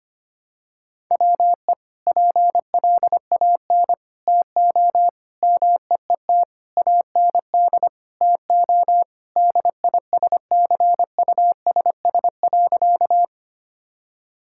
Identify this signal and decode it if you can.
Morse code, heard as a keyed tone, about 25 wpm, 700 Hz. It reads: WE PLAN TO MEET AND TO DISCUSS.
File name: practice / 25wpm / Sentences with 2-Word Phrases - 25wpm